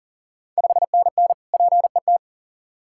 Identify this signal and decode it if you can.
Morse code, heard as a keyed tone, about 38 wpm, 690 Hz. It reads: 5NN PET
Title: Ontario QSO Party Exchange 40wpm